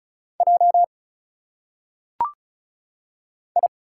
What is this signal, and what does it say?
Morse code, heard as a keyed tone, about 36 wpm, 700 Hz. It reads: J I
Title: Single Letters - Phonetic Pronunciation 35wpm